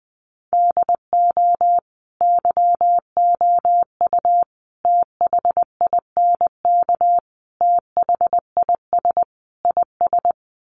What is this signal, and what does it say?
Morse code, heard as a keyed tone, about 20 wpm, 700 Hz. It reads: DO YOU THINK THIS IS
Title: Sets of 5 Words Encore2 20wpm